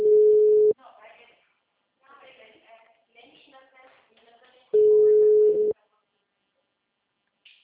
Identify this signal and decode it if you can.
Ringback tone (Europe, 425 Hz), with speech.
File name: out- -2008- - - .42229